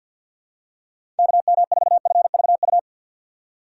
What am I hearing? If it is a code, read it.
Morse code, heard as a keyed tone, about 51 wpm, 700 Hz. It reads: KM4FVU
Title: USA Call Signs - 6 Characters - 3x Character Spacing - 50wpm